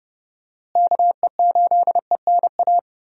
Morse code, heard as a keyed tone, about 30 wpm, 700 Hz. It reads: KE8ENA
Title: USA Call Signs - 6 Characters - Repeated 3x - 30wpm